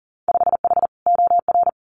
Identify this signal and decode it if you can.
Morse code, heard as a keyed tone, about 42 wpm, 700 Hz. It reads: 5H OR
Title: Winter Field Day Exchanges Encore 40wpm